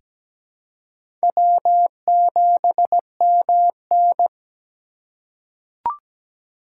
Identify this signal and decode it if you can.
Morse code, heard as a keyed tone, about 17 wpm, 700 Hz. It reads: W7MN
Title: USA Call Signs - 3x Character Spacing - 17wpm